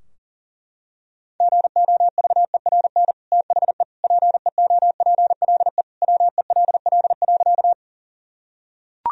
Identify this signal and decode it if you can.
Morse code, heard as a keyed tone, about 40 wpm, 700 Hz. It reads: GOVERN THE PEOPLE WELL.
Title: Sentences from Top 1000 Words - Review Part 2 - 40wpm